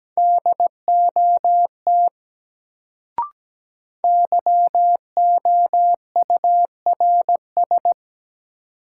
Morse code, heard as a keyed tone, about 17 wpm, 700 Hz. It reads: DOT YOURS
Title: Sentences from Top 1500 Words - New Word Drill - 17wpm